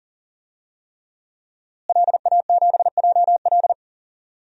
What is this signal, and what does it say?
Morse code, heard as a keyed tone, about 40 wpm, 700 Hz. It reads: LA7JL
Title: DXCC-100 Call Signs - 5 Characters - Repeated 3x - 40wpm